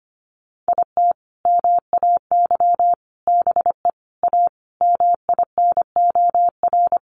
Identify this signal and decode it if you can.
Morse code, heard as a keyed tone, about 25 wpm, 700 Hz. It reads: IT MAY BE A MINOR
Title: Sentences from Top 1500 Words 25wpm